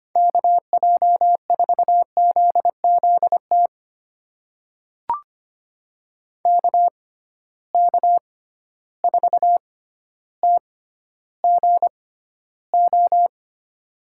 Morse code, heard as a keyed tone, about 25 wpm, spaced as for 13 wpm, 700 Hz. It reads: KJ4ZZT KK4TGO
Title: USA Call Signs - 6 Characters - 6x Character Spacing - 25wpm